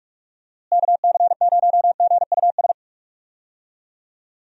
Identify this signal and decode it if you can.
Morse code, heard as a keyed tone, about 46 wpm, 700 Hz. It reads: KC0GUS
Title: USA Call Signs - 6 Characters - Repeated 2x - 45wpm